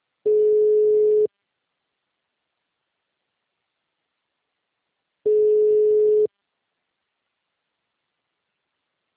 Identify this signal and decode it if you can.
Ringback tone (Europe, 425 Hz).
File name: out- -2039- - - .34205